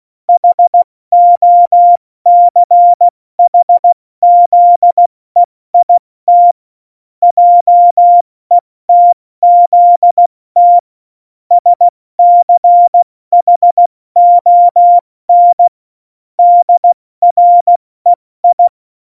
Morse code, heard as a keyed tone, about 16 wpm, 700 Hz. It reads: HOCHZEIT JETZT SCHON DREI
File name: Morsekurs / Witze / Witz 13. August 2019, 70 F80